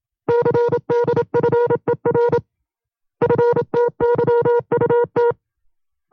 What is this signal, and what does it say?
Morse code, heard as a keyed tone, about 27 wpm, 480 Hz. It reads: CDFER FTYUT
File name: FreeFiles / CW-practice / 25WPM-side1